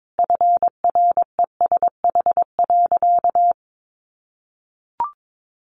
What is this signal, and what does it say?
Morse code, heard as a keyed tone, about 22 wpm, 700 Hz. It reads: FRESH.
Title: Sentences from Top 1200 Words 22wpm